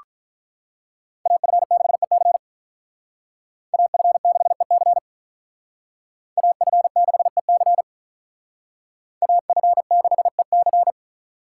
Morse code, heard as a keyed tone, about 45 wpm, 700 Hz. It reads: AF6EC AF6EC AF6EC AF6EC